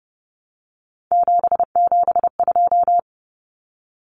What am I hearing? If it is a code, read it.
Morse code, heard as a keyed tone, about 30 wpm, 700 Hz. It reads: 772
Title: Numbers - 3-Digit 30wpm